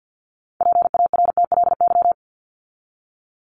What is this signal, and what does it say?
Morse code, heard as a keyed tone, about 50 wpm, 700 Hz. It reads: PARTLY
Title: LY Suffix - Top 100 Words with Base Words 50wpm